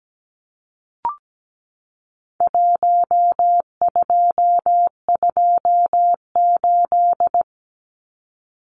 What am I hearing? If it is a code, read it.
Morse code, heard as a keyed tone, about 17 wpm, 700 Hz. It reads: 1228
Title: Numbers - Contest Style - 4-Digit 17wpm